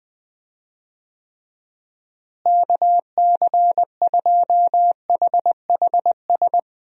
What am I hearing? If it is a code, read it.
Morse code, heard as a keyed tone, about 21 wpm, 700 Hz. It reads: KC2HHS
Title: USA Call Signs - 6 Characters - Repeated 2x - 20wpm